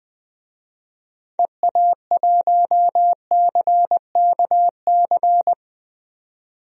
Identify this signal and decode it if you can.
Morse code, heard as a keyed tone, about 20 wpm, 700 Hz. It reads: EA1CKC
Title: DXCC-100 Call Signs - 6 Characters - 20wpm